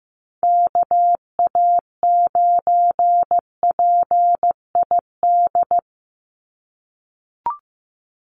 Morse code, heard as a keyed tone, about 15 wpm, 700 Hz. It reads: KA9PID
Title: USA Call Signs 15wpm